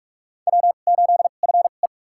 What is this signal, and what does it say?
Morse code, heard as a keyed tone, about 46 wpm, spaced as for 33 wpm, 700 Hz. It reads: W8FE
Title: USA Call Signs - 4 Characters - 2x Character Spacing - 45wpm